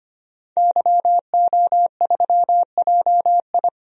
Morse code, heard as a keyed tone, about 25 wpm, 700 Hz. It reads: YO3JI
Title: DXCC-100 Call Signs - 5 Characters - 25wpm